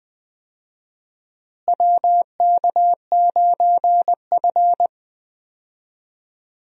Morse code, heard as a keyed tone, about 20 wpm, 700 Hz. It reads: WK9F
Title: USA Call Signs - 4 Characters - Repeated 3x - 20wpm